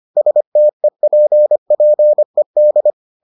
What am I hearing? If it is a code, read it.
Morse code, heard as a keyed tone, about 25 wpm, 600 Hz. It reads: STEPPED